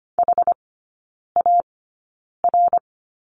Morse code, heard as a keyed tone, about 25 wpm, spaced as for 9 wpm, 700 Hz. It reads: HAR
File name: Lesson 15e - Number 5 - Three Characters 25wpm